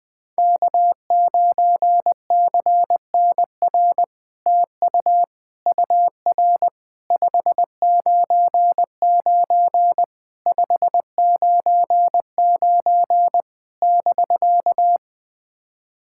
Morse code, heard as a keyed tone, about 21 wpm, 700 Hz. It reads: K9CNR TU UR 599 599 <BK>
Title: SOTA QSO Practice 20wpm